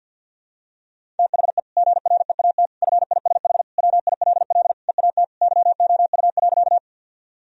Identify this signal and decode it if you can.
Morse code, heard as a keyed tone, about 50 wpm, 700 Hz. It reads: THE GREAT FISH WILL EAT YOU.
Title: Sentences from Top 1000 Words - Review Part 1 - 50wpm